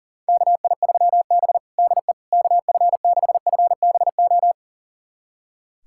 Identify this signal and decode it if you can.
Morse code, heard as a keyed tone, about 37 wpm, 700 Hz. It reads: KI3B DE KF6FBO
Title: USA Callsign DE Callsign 40wpm